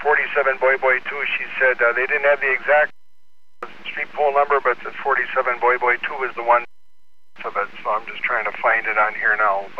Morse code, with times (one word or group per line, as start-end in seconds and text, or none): none